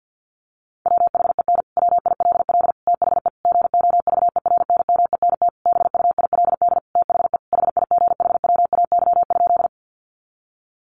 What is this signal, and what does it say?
Morse code, heard as a keyed tone, about 50 wpm, 700 Hz. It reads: WHEN WILL THE GOVERNMENT BUILD THE HIGHWAY?
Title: Sentences from Top 1400 Words 50wpm